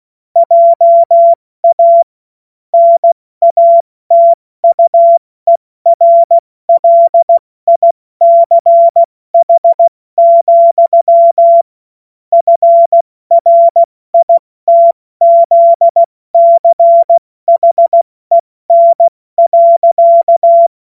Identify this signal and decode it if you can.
Morse code, heard as a keyed tone, about 16 wpm, 680 Hz. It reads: JA NATUERLICH, FRITZCHEN.
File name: Morsekurs / Witze / Witz 21. September 2021, 70 F80